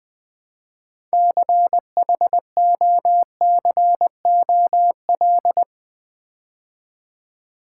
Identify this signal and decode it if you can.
Morse code, heard as a keyed tone, about 20 wpm, 700 Hz. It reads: CHOCOL
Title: Build A Word - 3 Letters At A Time - 9-Letter Words - 20wpm